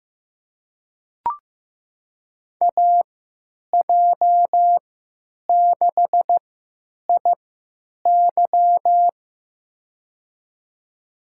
Morse code, heard as a keyed tone, about 15 wpm, spaced as for 9 wpm, 700 Hz. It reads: AJ6IY
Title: USA Call Signs - 5 Characters - 3x Character Spacing - 15wpm